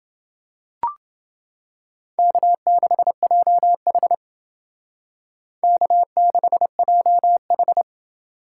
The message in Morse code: K6JH K6JH

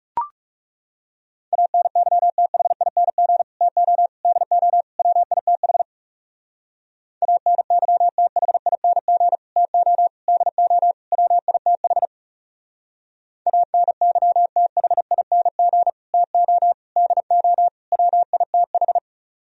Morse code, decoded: ANYTHING TO DO WITH ANYTHING TO DO WITH ANYTHING TO DO WITH

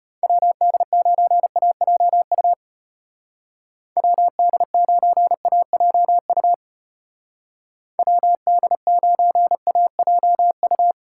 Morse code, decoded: WD9AJU WD9AJU WD9AJU

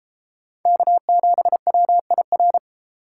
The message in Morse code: K7WIR

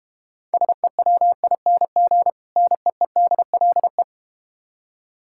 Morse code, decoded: SEWING NEEDLE

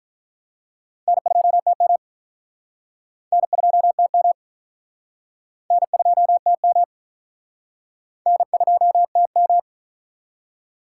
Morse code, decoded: N2TM N2TM N2TM N2TM

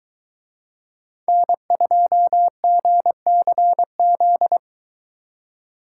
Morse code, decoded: N2GCZ

